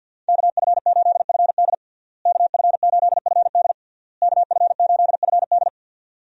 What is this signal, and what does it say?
Morse code, heard as a keyed tone, about 48 wpm, 700 Hz. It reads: KF8FD KF8FD KF8FD
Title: USA Call Signs - 5 Characters - Repeated 3x - 50wpm